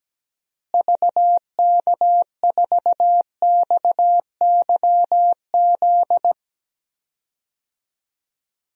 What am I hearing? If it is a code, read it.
Morse code, heard as a keyed tone, about 17 wpm, 700 Hz. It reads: VK4XYZ
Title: DXCC-100 Call Signs - 6 Characters - Repeated 3x - 17wpm